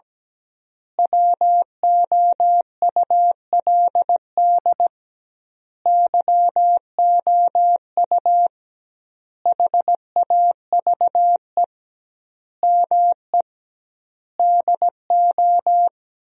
Morse code, heard as a keyed tone, about 17 wpm, 700 Hz. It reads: WOULD YOU HAVE ME DO
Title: 2x Word-Spacing - Sentences from Top 300 Words - 17wpm